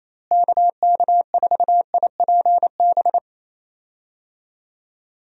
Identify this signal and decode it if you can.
Morse code, heard as a keyed tone, about 28 wpm, 700 Hz. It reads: KK4IPB
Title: USA Call Signs - 6 Characters - Repeated 2x - 28wpm